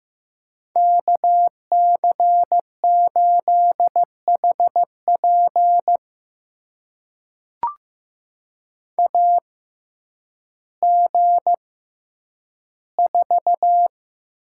Morse code, heard as a keyed tone, about 15 wpm, spaced as for 9 wpm, 700 Hz. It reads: KC8HP AG4
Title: USA Call Signs - 5 Characters - 6x Character Spacing - 15wpm